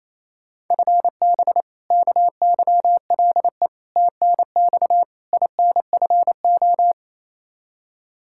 Morse code, heard as a keyed tone, about 28 wpm, 700 Hz. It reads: FB KYLE TNX INFO
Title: Practice QSO - Exchange 3 - More Info - 28wpm